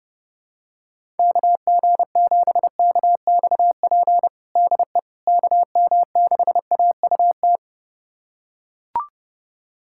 Morse code, heard as a keyed tone, about 30 wpm, 700 Hz. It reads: KG7KXP DE KM6AUT
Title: USA Callsign DE Callsign - 6 Characters - 30wpm